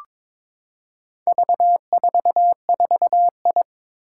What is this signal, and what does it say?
Morse code, heard as a keyed tone, about 22 wpm, 700 Hz. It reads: V44I
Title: Four Challenging Letter-Number Combinations 22wpm